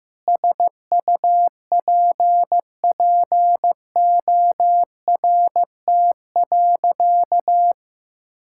Morse code, heard as a keyed tone, about 15 wpm, 700 Hz. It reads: SUPPORT.
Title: Sentences from Top 1000 Words - Review Part 5 - 15wpm